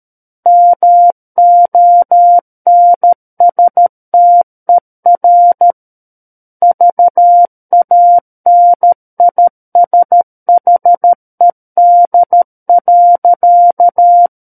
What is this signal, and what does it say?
Morse code, heard as a keyed tone, about 13 wpm, 700 Hz. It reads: MONSTER VANISHED.
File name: Book One / 04-Chapter 4 - 13 WPM